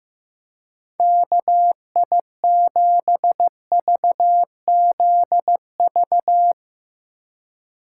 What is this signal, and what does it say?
Morse code, heard as a keyed tone, about 15 wpm, 700 Hz. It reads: KI7VZV
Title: USA Call Signs - 6 Characters - 3x Character Spacing - 15wpm